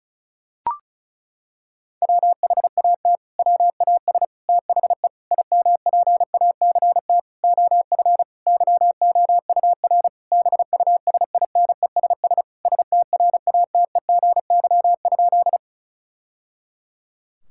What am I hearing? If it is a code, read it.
Morse code, heard as a keyed tone, about 36 wpm, 700 Hz. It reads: WHAT WAS THE IMPACT OF YOUR BUSINESS STRATEGY?
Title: Sentences from Top 1000 Words - Review Part 6 - 35wpm